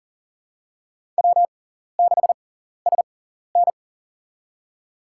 Morse code, heard as a keyed tone, about 40 wpm, spaced as for 14 wpm, 690 Hz. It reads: W6SN
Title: USA Call Signs - 6x Character Spacing - 40wpm